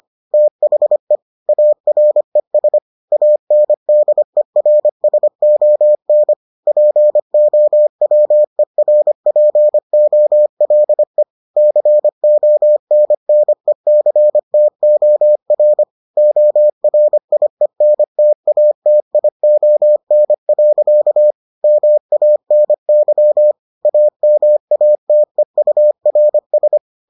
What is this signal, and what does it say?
Morse code, heard as a keyed tone, about 25 wpm, 600 Hz. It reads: THE ARES ANDERSON POWERPOLE CONNECTOR ORIENTATION. MANY AMATEURS